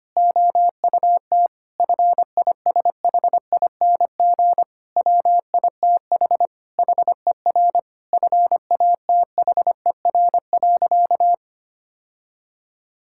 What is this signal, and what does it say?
Morse code, heard as a keyed tone, about 25 wpm, 700 Hz. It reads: OUT FISHING WITH HER FATHER.
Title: Sentences from Top 1000 Words - Review Part 5 - 25wpm